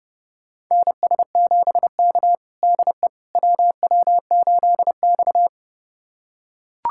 Morse code, heard as a keyed tone, about 30 wpm, 700 Hz. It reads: NS7K DE WW8X
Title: USA Callsign DE Callsign - 4 Characters - 30wpm